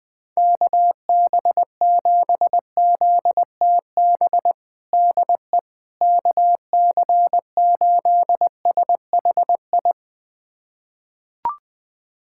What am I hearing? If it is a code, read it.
Morse code, heard as a keyed tone, about 21 wpm, 700 Hz. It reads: KB7ZTB DE KC8SHI
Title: USA Callsign DE Callsign - 6 Characters - 20wpm